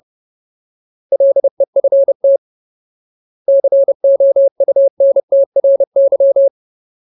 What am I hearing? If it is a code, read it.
Morse code, heard as a keyed tone, about 30 wpm, 550 Hz. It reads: LEFT COUNTRY
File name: Ultimate Word List / 30WPM Ultimate